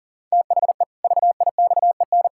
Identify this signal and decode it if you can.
Morse code, heard as a keyed tone, about 40 wpm, 700 Hz. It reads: THE VIXEN
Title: Pangram Challenge v2 - 40wpm